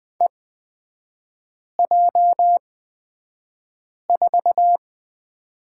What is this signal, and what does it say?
Morse code, heard as a keyed tone, about 20 wpm, spaced as for 5 wpm, 700 Hz. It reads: EJ4
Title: 224 Lesson 36e - BT - Three Characters 20wpm